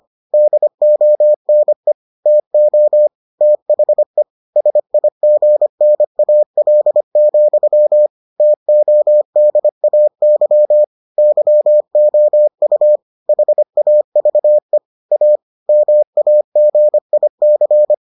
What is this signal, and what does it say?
Morse code, heard as a keyed tone, about 25 wpm, 600 Hz. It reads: DONE TO THE SIGNAL, TODAY YOU HAVE A MAGIC